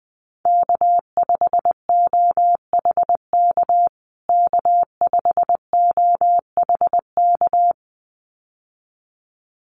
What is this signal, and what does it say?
Morse code, heard as a keyed tone, about 20 wpm, 700 Hz. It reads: K5OHK K5OHK
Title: USA Call Signs Repeated 2x 20wpm